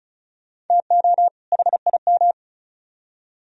Morse code, heard as a keyed tone, about 47 wpm, 690 Hz. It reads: TO HIM